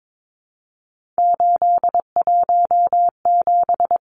Morse code, heard as a keyed tone, about 22 wpm, 700 Hz. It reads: 817
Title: Numbers - 3-Digit 22wpm